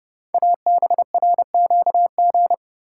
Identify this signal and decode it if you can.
Morse code, heard as a keyed tone, about 30 wpm, 700 Hz. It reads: ABRQG
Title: Code Groups - 5 Letters - 30wpm